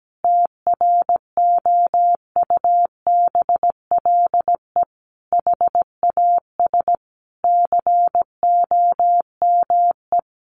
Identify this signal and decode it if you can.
Morse code, heard as a keyed tone, about 17 wpm, 700 Hz. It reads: TROUBLE HAS COME
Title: Sentences from Top 1200 Words 17wpm